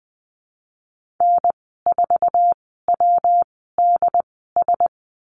Morse code, heard as a keyed tone, about 20 wpm, spaced as for 15 wpm, 700 Hz. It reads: N4WDS
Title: 122 Lesson 20g - Number 3 - Call Signs 20wpm